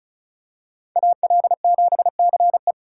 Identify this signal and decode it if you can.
Morse code, heard as a keyed tone, about 34 wpm, 700 Hz. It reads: AL7CE
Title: USA Call Signs - 5 Characters - 3x Character Spacing - 35wpm